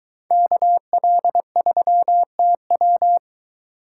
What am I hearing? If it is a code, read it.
Morse code, heard as a keyed tone, about 23 wpm, 700 Hz. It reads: KL3TW